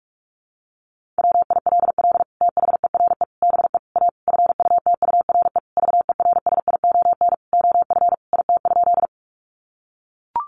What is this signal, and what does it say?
Morse code, heard as a keyed tone, about 44 wpm, 700 Hz. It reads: WILL THERE BE A FUTURE VERSION OF IT?
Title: Sentences from Top 1400 Words 45wpm